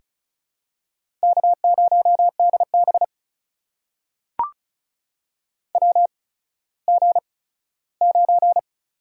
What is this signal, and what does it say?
Morse code, heard as a keyed tone, about 36 wpm, spaced as for 17 wpm, 700 Hz. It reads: K0DB WG9
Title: USA Call Signs - 4 Characters - 8x Character Spacing - 35wpm